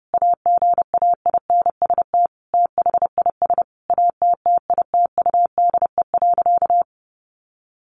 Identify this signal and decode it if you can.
Morse code, heard as a keyed tone, about 30 wpm, 700 Hz. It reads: AGAINST THIS ATTITUDE.
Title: Sentences from Top 1300 Words 30wpm